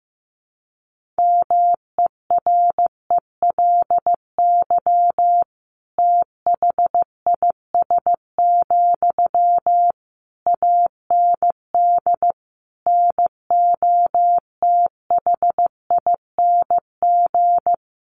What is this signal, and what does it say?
Morse code, heard as a keyed tone, about 14 wpm, 700 Hz. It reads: MERELY THIS, AND NOTHING
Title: Sentences from Top 1400 Words 15wpm